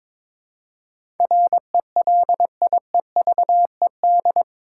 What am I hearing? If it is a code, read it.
Morse code, heard as a keyed tone, about 22 wpm, 700 Hz. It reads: RELIEVED
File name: practice / 22wpm / Eight Letter Words - 22wpm